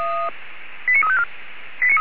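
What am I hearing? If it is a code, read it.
Five-tone selective call (ZVEI), heard as quick runs of five tones, tones of about 70 ms. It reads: 89354
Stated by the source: Five-tone ZVEI selective call received on an SDR by feuerwehr-orlamuende, fire brigade alerting.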